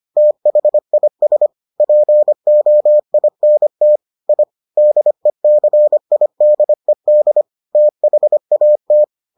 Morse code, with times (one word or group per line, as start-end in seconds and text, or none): 0.2-1.5 THIS
1.8-4.0 POINT
4.3-4.4 I
4.8-7.4 DECIDED
7.8-9.0 THAT